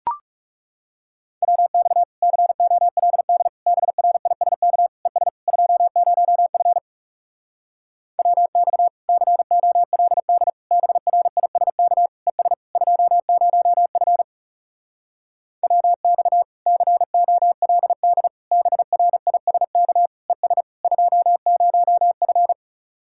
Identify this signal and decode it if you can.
Morse code, heard as a keyed tone, about 40 wpm, 700 Hz. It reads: WX COLD BRISK ES 20F WX COLD BRISK ES 20F WX COLD BRISK ES 20F